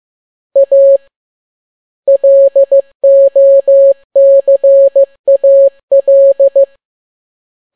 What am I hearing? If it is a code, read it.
Morse code, heard as a keyed tone, about 15 wpm, 550 Hz. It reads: A LOCAL
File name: n6tt p1 15C13